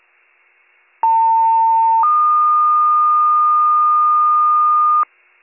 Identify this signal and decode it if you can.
Two-tone page, heard as a one-second tone, then a three-second tone, about 900 then 1220 Hz. It is heard from 1.0 to 5.0 s.